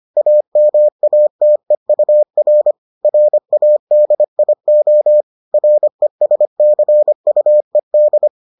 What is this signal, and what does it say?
Morse code, heard as a keyed tone, about 25 wpm, 600 Hz. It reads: AMATEUR RADIO RESCUED